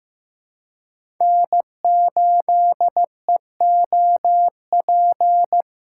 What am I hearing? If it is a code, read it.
Morse code, heard as a keyed tone, about 16 wpm, 700 Hz. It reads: N8EOP
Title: USA Call Signs - 5 Characters - 2x Character Spacing - 15wpm